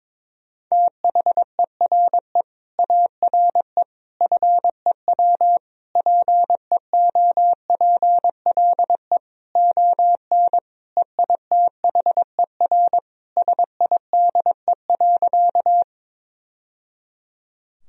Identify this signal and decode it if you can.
Morse code, heard as a keyed tone, about 21 wpm, 700 Hz. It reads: THERE ARE FEW PEOPLE ON EITHER SIDE.
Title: Sentences from Top 1000 Words - Review Part 4 - 22wpm